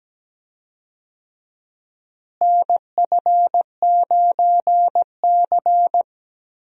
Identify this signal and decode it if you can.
Morse code, heard as a keyed tone, about 17 wpm, 700 Hz. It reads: NF9C
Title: USA Call Signs - 4 Characters - Repeated 3x - 17wpm